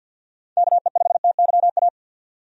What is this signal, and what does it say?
Morse code, heard as a keyed tone, about 52 wpm, 700 Hz. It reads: KE5TYA
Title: USA Call Signs - 6 Characters - 50wpm